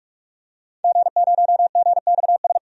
Morse code, heard as a keyed tone, about 46 wpm, 700 Hz. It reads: G0GXS
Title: DXCC-100 Call Signs - 5 Characters - 45wpm